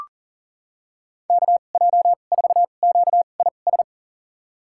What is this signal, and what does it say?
Morse code, heard as a keyed tone, about 37 wpm, spaced as for 28 wpm, 700 Hz. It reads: KJ4QIS
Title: USA Call Signs - 2x Character Spacing - 40wpm